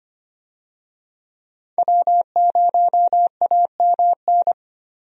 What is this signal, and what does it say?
Morse code, heard as a keyed tone, about 25 wpm, 700 Hz. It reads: W0AMN
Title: USA Call Signs - 5 Characters - 25wpm